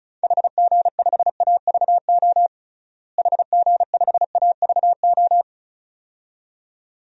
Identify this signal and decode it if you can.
Morse code, heard as a keyed tone, about 36 wpm, 700 Hz. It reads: HG5AVO HG5AVO
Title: DXCC-100 Call Signs - 6 Characters - Repeated 2x - 35wpm